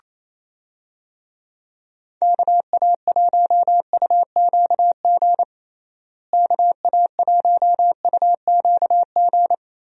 Morse code, heard as a keyed tone, about 27 wpm, 700 Hz. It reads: KA1UQG KA1UQG
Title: USA Call Signs - 6 Characters - Repeated 3x - 28wpm